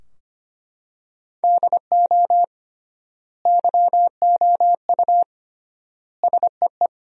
Morse code, heard as a keyed tone, about 25 wpm, 700 Hz. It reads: DO YOU SEE
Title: 3x Word-Spacing - Sentences from Top 100 Words - 25wpm